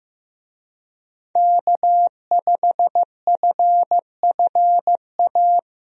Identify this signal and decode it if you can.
Morse code, heard as a keyed tone, about 16 wpm, 700 Hz. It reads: K5FFA
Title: USA Call Signs - 5 Characters - Repeated 3x - 15wpm